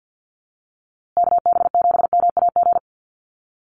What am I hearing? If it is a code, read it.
Morse code, heard as a keyed tone, about 50 wpm, 700 Hz. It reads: KB7MAG
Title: USA Call Signs Repeated 3x 50wpm